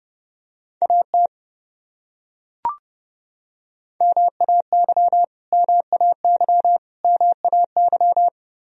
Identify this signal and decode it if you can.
Morse code, heard as a keyed tone, about 30 wpm, 700 Hz. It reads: AT MAY MAY MAY
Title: Top 100 Words Repeated 30wpm